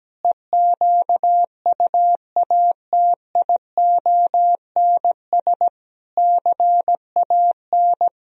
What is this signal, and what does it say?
Morse code, heard as a keyed tone, about 17 wpm, 700 Hz. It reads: EQUATIONS CAN
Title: Speed-Racing - Sets of 5 Words 17wpm